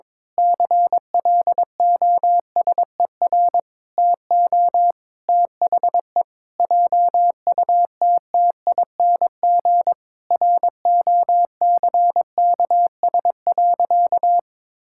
Morse code, heard as a keyed tone, about 22 wpm, 700 Hz. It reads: CLOSER TO THE JUTTING ROCKS.
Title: Pangram Challenge v2 - 22wpm